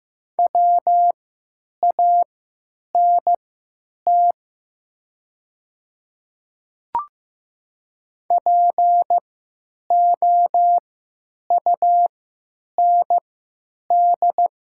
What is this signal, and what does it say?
Morse code, heard as a keyed tone, about 15 wpm, spaced as for 9 wpm, 700 Hz. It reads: WANT POUND
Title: 3x Character Spacing - Top 500 Words - 15wpm